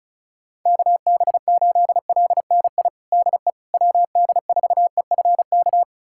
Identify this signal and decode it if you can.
Morse code, heard as a keyed tone, about 36 wpm, 700 Hz. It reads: KB8LNI DE WD4EFK
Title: USA Callsign DE Callsign 35wpm